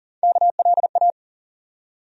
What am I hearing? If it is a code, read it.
Morse code, heard as a keyed tone, about 40 wpm, 700 Hz. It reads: KLA
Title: USA Call Sign Postfixes - 3 Letters 40wpm